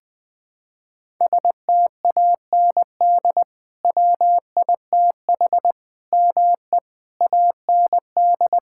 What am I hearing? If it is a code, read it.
Morse code, heard as a keyed tone, about 21 wpm, 700 Hz. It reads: STAND WITH ME AND
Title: Sentences from Top 1600 Words 20wpm